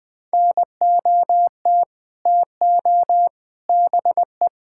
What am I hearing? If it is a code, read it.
Morse code, heard as a keyed tone, about 20 wpm, 700 Hz. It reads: NOT TO BE
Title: Sets of 6 Words - 20wpm